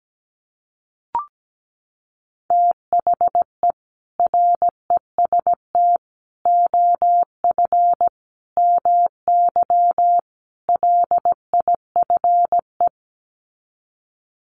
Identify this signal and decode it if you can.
Morse code, heard as a keyed tone, about 17 wpm, 700 Hz. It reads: THE REST OF MY LIFE
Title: Sets of 5 Words 17wpm